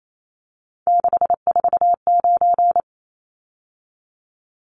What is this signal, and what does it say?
Morse code, heard as a keyed tone, about 28 wpm, 700 Hz. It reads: 649
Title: Numbers - 3-Digit 28wpm